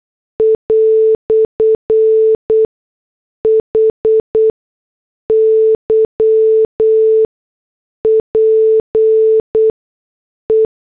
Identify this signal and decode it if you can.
Morse code, heard as a keyed tone, about 8 wpm, spaced as for 6 wpm, 440 Hz. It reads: "HYPE
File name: morse / headlines